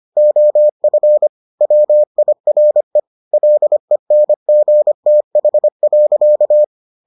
Morse code, heard as a keyed tone, about 25 wpm, 600 Hz. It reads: OF WIRE LENGTH.